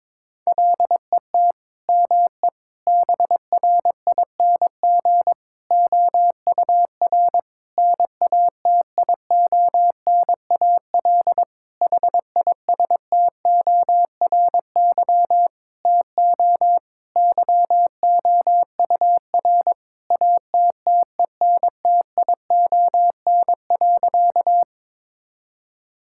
Morse code, heard as a keyed tone, about 22 wpm, 700 Hz. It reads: LET ME BRING OUR NATIONAL HISTORY TO YOUR ATTENTION.
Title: Sentences from Top 1000 Words - Review Part 6 - 22wpm